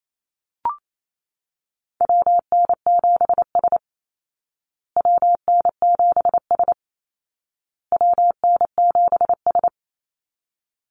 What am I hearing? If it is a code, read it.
Morse code, heard as a keyed tone, about 28 wpm, 700 Hz. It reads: WN7S WN7S WN7S
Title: Call Signs Repeated 2x - 28wpm